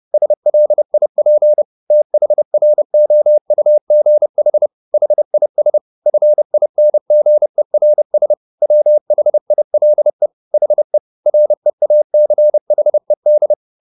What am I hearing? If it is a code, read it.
Morse code, heard as a keyed tone, about 30 wpm, 600 Hz. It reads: SLIP THROUGH HIS FINGERS WHILE HE REACHED
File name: the old man and the sea / the old man and the sea 1 30wpm